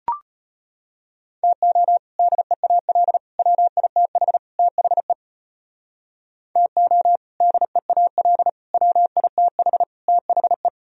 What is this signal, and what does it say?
Morse code, heard as a keyed tone, about 38 wpm, 700 Hz. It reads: TO DEAL WITH THE TO DEAL WITH THE